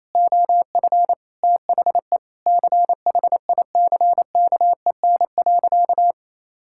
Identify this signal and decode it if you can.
Morse code, heard as a keyed tone, about 27 wpm, 700 Hz. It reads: OF THE CHICKEN.
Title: Sentences from Top 1000 Words - Review Part 5 - 28wpm